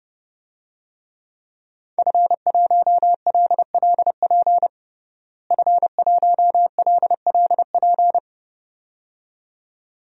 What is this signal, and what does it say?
Morse code, heard as a keyed tone, about 32 wpm, 700 Hz. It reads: F1LLP F1LLP
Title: DXCC-100 Call Signs - 5 Characters - Repeated 2x - 30wpm